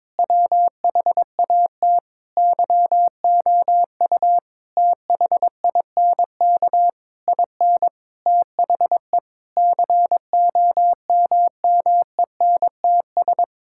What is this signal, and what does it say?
Morse code, heard as a keyed tone, about 22 wpm, 700 Hz. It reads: WHAT YOU THINK IN THE COMMENTS
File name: Sentences with 9-Word Phrases - 22wpm